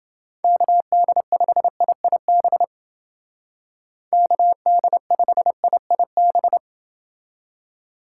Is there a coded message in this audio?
Yes: Morse code from 0.4 to 6.6 s, heard as a keyed tone, about 30 wpm, 700 Hz. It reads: KD5IIB KD5IIB